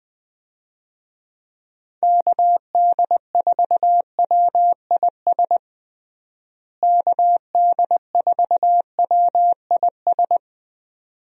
Morse code, heard as a keyed tone, about 20 wpm, 700 Hz. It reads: KD4WIS KD4WIS
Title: USA Call Signs - 6 Characters - Repeated 2x - 20wpm